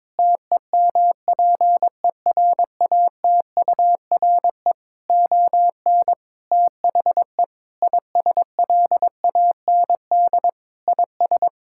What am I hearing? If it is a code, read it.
Morse code, heard as a keyed tone, about 22 wpm, 700 Hz. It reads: TEMPERATURE ON THE ISLAND IS
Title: Sentences from Top 1000 Words - Review Part 6 - 22wpm